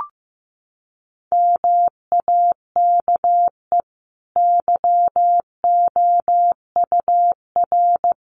Morse code, heard as a keyed tone, about 15 wpm, 700 Hz. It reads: MAKE YOUR
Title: Sentences from Top 900 Words 15wpm